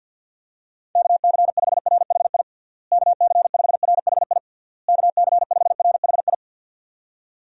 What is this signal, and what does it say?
Morse code, heard as a keyed tone, about 52 wpm, 700 Hz. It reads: KC5RHI KC5RHI KC5RHI
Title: USA Call Signs - 6 Characters - Repeated 3x - 50wpm